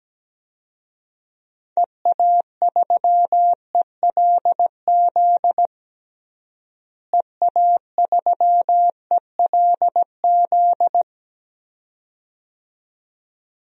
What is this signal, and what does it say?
Morse code, heard as a keyed tone, about 17 wpm, 700 Hz. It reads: EA3ELZ EA3ELZ
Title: DXCC-100 Call Signs - 6 Characters - Repeated 2x - 17wpm